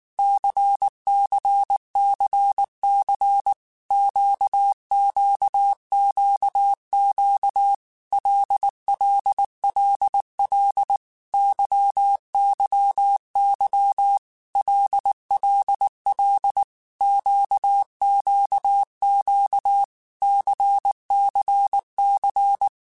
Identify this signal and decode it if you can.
Morse code, heard as a keyed tone, about 19 wpm, 800 Hz. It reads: CCCC QQQQ LLLL YYY LLL QQQ CCC
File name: Groups@18 / Groups@18 mp3 / LFCYQ@18